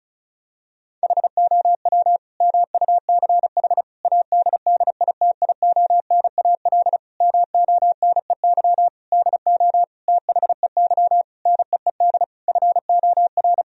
Morse code, heard as a keyed tone, about 34 wpm, 700 Hz. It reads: HOW MUCH ADDITIONAL MONEY DO THEY NEED FOR
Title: Sentences from Top 1000 Words - Review Part 4 - 35wpm